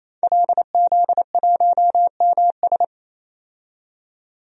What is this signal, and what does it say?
Morse code, heard as a keyed tone, about 28 wpm, 700 Hz. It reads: LZ1MS
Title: DXCC-100 Call Signs - 5 Characters - 28wpm